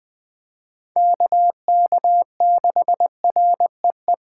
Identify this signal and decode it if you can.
Morse code, heard as a keyed tone, about 20 wpm, 700 Hz. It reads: KK6REE
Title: Speed Racing - USA 2x3 Call Signs 17wpm